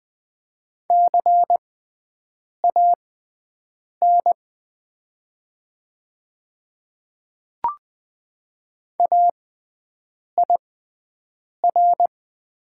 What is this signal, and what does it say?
Morse code, heard as a keyed tone, about 21 wpm, spaced as for 7 wpm, 700 Hz. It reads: CAN AIR
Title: 6x Character Spacing - Top 200 Words - 20wpm